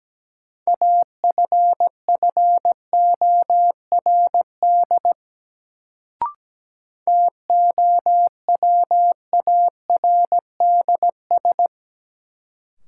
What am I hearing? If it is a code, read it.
Morse code, heard as a keyed tone, about 17 wpm, 700 Hz. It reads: AFFORD TOWARDS
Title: Sentences from Top 1100 Words - New Word Drill Alt - 17wpm